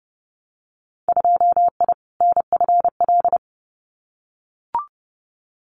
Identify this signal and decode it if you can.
Morse code, heard as a keyed tone, about 30 wpm, 700 Hz. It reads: 2I NFL
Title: Winter Field Day Exchanges 30wpm